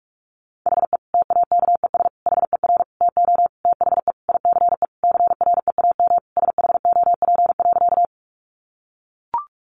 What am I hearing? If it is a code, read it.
Morse code, heard as a keyed tone, about 44 wpm, 700 Hz. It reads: HE TAKES HER TO THE ICE CREAM SHOP.
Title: Additional Easy Phrases 45wpm